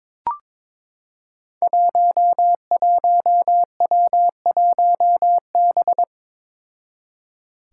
Morse code, heard as a keyed tone, about 22 wpm, 700 Hz. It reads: 11W1B
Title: Five Challenging Letter-Number Combinations 22wpm